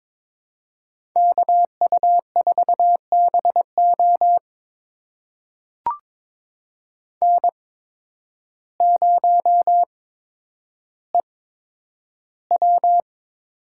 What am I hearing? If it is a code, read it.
Morse code, heard as a keyed tone, about 22 wpm, spaced as for 10 wpm, 700 Hz. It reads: KU4BO N0EW
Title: USA Call Signs - 5 Characters - 8x Character Spacing - 22wpm